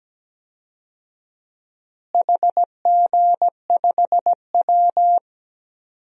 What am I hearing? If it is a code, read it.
Morse code, heard as a keyed tone, about 17 wpm, 700 Hz. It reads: HG5W
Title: DXCC-100 Call Signs - 4 Characters - Repeated 2x - 17wpm